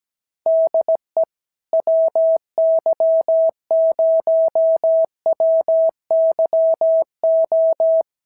Morse code, heard as a keyed tone, about 17 wpm, 650 Hz. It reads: DE WY0WYO
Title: SOTA QSO Practice 17wpm